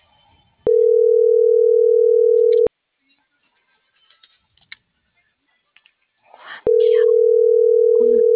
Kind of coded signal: telephone call progress tone

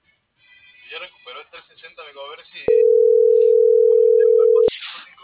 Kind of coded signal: telephone call progress tone

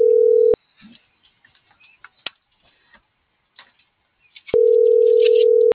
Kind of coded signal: telephone call progress tone

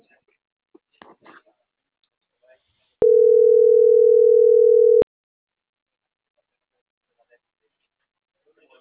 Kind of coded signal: telephone call progress tone

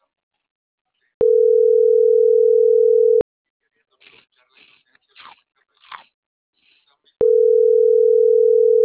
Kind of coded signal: telephone call progress tone

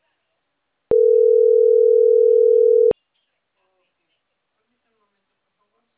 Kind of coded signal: telephone call progress tone

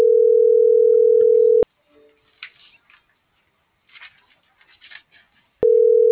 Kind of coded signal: telephone call progress tone

